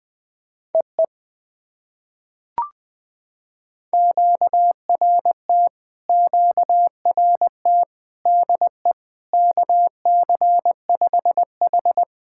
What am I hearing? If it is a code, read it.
Morse code, heard as a keyed tone, about 20 wpm, 700 Hz. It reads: EE QRT QRT DE KC5H
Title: POTA QSO Practice - 20wpm